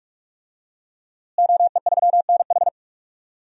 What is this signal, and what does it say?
Morse code, heard as a keyed tone, about 46 wpm, 700 Hz. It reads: OE3NH
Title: DXCC-100 Call Signs - 5 Characters - Repeated 2x - 45wpm